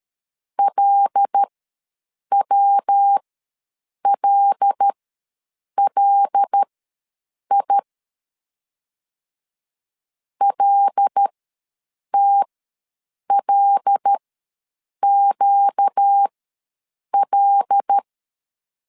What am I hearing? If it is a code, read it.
Morse code, heard as a keyed tone, about 13 wpm, spaced as for 7 wpm, 800 Hz. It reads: LWLLI LTLQL
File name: LEZIONI / 5 LBVC / 5 L ws 12wpm 2958